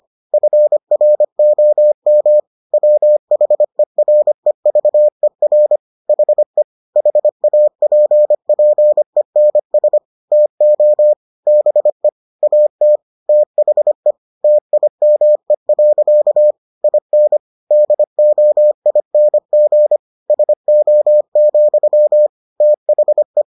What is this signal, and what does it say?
Morse code, heard as a keyed tone, about 25 wpm, 600 Hz. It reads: FROM WHEREVER HE HAPPENS TO BE AT THE TIME. IN DOING SO, THE